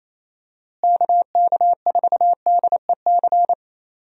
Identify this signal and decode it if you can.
Morse code, heard as a keyed tone, about 27 wpm, 700 Hz. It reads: KK4DEC
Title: USA Call Signs - 6 Characters - Repeated 2x - 28wpm